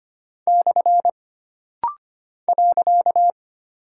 Morse code, heard as a keyed tone, about 25 wpm, 700 Hz. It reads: / .
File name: Single Punctuation-Prosign - Rapid-Fire - 25wpm